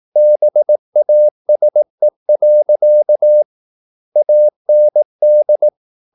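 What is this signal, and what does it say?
Morse code, heard as a keyed tone, about 18 wpm, 600 Hz. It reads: BASE. AND
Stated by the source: Morse practice at 18 wpm, English text, from CW med Gnister.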